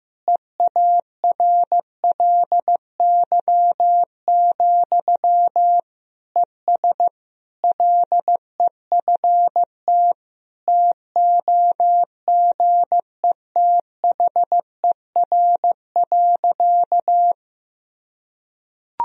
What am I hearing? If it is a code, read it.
Morse code, heard as a keyed tone, about 15 wpm, 700 Hz. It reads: EARLY, ES LEFT TOGETHER.
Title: Sentences from Top 400 Words and Abbreviations 15wpm